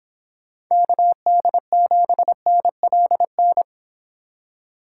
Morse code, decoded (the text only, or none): KD7NLN